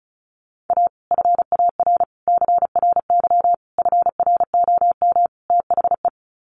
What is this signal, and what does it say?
Morse code, heard as a keyed tone, about 34 wpm, 700 Hz. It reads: A FAR CRY FROM THE
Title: Sets of 5 Words Encore 35wpm